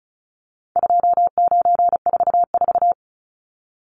Morse code, heard as a keyed tone, about 34 wpm, 700 Hz. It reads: 2944
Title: Numbers - 4-Digit 35wpm